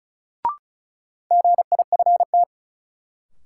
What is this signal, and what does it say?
Morse code, heard as a keyed tone, about 36 wpm, 700 Hz. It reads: GIFT
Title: Sentences from Top 900 Words - New Word Drill - 35wpm